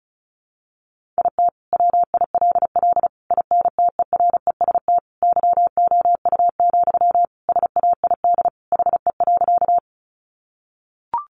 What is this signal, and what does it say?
Morse code, heard as a keyed tone, about 34 wpm, 700 Hz. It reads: IT WILL INTEREST YOU, SAID HE.
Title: Sentences from Top 400 Words 35wpm